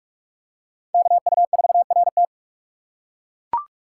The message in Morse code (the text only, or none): KU4RT